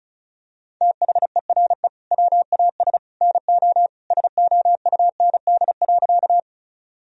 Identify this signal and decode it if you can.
Morse code, heard as a keyed tone, about 36 wpm, 700 Hz. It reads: THERE WAS NO SOUND.
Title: Sentences from Top 1000 Words - Review Part 1 - 35wpm